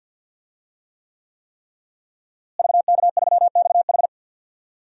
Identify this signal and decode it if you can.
Morse code, heard as a keyed tone, about 45 wpm, 700 Hz. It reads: VK3XH
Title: DXCC-100 Call Signs - 5 Characters - Repeated 3x - 50wpm